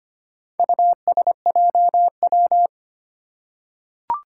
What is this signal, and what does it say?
Morse code, heard as a keyed tone, about 25 wpm, 700 Hz. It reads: USJW
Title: Four Challenging Letter-Number Combinations 25wpm